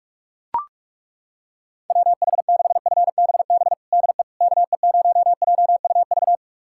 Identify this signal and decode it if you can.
Morse code, heard as a keyed tone, about 42 wpm, 700 Hz. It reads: WH6FBB DE KE0JUV
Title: USA Callsign DE Callsign 45wpm